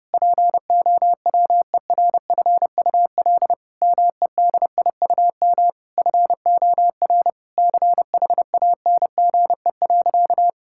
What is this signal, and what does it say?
Morse code, heard as a keyed tone, about 32 wpm, 700 Hz. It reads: POWERFUL MEDIUM FOR CHANGE.
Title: Sentences from Top 1000 Words - Review Part 6 - 30wpm